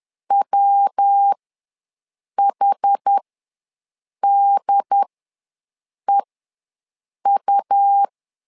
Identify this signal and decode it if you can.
Morse code, heard as a keyed tone, about 10 wpm, spaced as for 6 wpm, 800 Hz. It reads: WHDEU
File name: LEZIONI / 3 MWRO / 3 MW ws 10wpm 2213